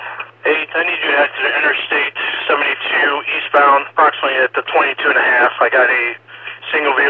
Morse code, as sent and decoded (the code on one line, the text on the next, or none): none